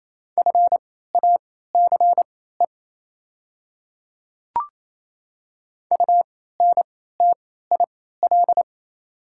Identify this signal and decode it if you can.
Morse code, heard as a keyed tone, about 28 wpm, spaced as for 16 wpm, 700 Hz. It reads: FACE UNTIL
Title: 3x Character Spacing - Top 300 Words - 28wpm